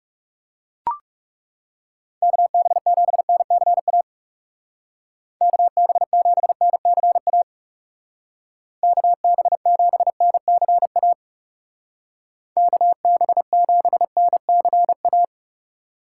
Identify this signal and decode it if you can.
Morse code, heard as a keyed tone, about 38 wpm, 700 Hz. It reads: KB7NCA KB7NCA KB7NCA KB7NCA